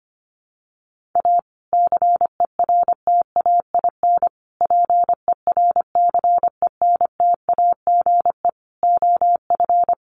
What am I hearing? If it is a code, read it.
Morse code, heard as a keyed tone, about 25 wpm, 700 Hz. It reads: A CERTAIN PERCENTAGE OF
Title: Sentences from Top 1300 Words 25wpm